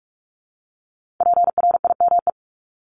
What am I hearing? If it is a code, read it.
Morse code, heard as a keyed tone, about 46 wpm, 700 Hz. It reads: PRIME
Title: Multi-letter Phonemes Repeated 3x 45wpm